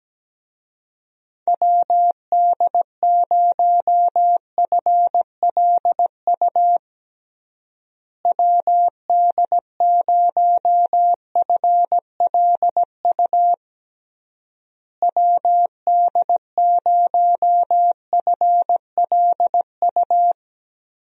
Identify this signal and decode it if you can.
Morse code, heard as a keyed tone, about 17 wpm, 700 Hz. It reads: WD0FLU WD0FLU WD0FLU
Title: USA Call Signs - 6 Characters - Repeated 3x - 17wpm